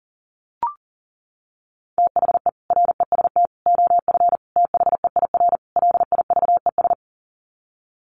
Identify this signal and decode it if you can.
Morse code, heard as a keyed tone, about 40 wpm, 700 Hz. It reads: THE REST OF THEIR LIVES
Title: Sets of 5 Words Encore 40wpm